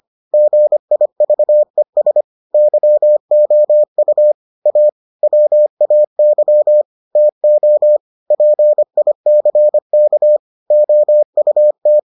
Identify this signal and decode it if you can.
Morse code, heard as a keyed tone, about 25 wpm, 600 Hz. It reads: GIVES YOU A WAY TO PICK OUT